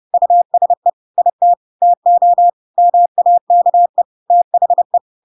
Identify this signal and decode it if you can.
Morse code, heard as a keyed tone, about 30 wpm, 700 Hz. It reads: USE IT TO MAKE THE
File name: Pinocchio / Pinocchio ch1 30 30 ik1hge